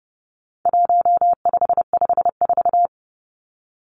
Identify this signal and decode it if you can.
Morse code, heard as a keyed tone, about 30 wpm, 700 Hz. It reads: 1554
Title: Numbers - 4-Digit 30wpm